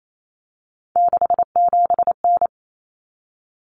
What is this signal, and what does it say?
Morse code, heard as a keyed tone, about 28 wpm, 700 Hz. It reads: 67N
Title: Common Cut Numbers - 3-Digit 28wpm